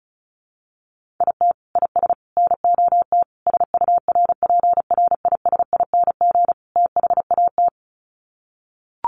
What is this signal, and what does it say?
Morse code, heard as a keyed tone, about 34 wpm, 700 Hz. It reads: IT IS NOT SURPRISING THAT
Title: Sets of 5 Words 35wpm